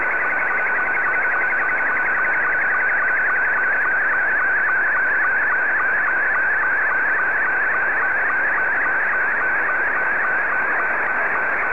SSTV (slow-scan television), Scottie 2.
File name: sstv / S2-TK7AA